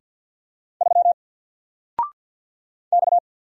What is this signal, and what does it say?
Morse code, heard as a keyed tone, about 48 wpm, 700 Hz. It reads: 3 X
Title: Single Letter-Number Alt 50wpm